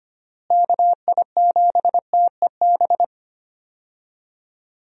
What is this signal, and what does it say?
Morse code, heard as a keyed tone, about 25 wpm, 700 Hz. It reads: KI7TEB
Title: USA Call Signs - 6 Characters - Repeated 3x - 25wpm